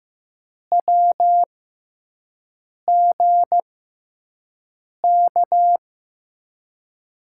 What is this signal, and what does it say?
Morse code, heard as a keyed tone, about 15 wpm, spaced as for 5 wpm, 700 Hz. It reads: WGK
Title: USA Call Signs - 6 Characters - 6x Character Spacing - 15wpm